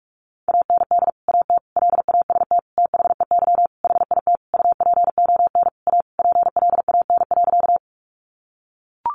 Morse code, heard as a keyed tone, about 44 wpm, 700 Hz. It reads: AND AT LAST THEY HIT UPON A PLAN.
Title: Sentences from Top 1200 Words 45wpm